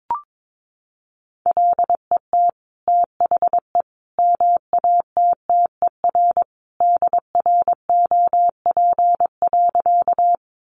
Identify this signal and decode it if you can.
Morse code, heard as a keyed tone, about 22 wpm, 700 Hz. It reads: LET THE MATTER DROP.
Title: Sentences from Top 1400 Words 22wpm